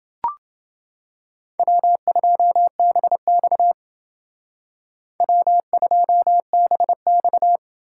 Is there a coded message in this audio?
Yes: Morse code from 1.6 to 7.6 s, heard as a keyed tone, about 29 wpm, 700 Hz. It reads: W2BX W2BX